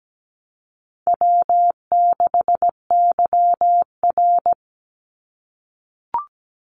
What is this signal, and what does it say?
Morse code, heard as a keyed tone, about 17 wpm, 700 Hz. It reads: W6YR
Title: Easy Call Signs - 17wpm